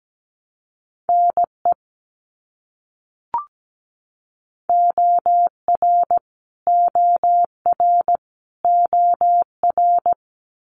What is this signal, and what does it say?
Morse code, heard as a keyed tone, about 17 wpm, 700 Hz. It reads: NE OR OR OR
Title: US States Repeated 3x 17wpm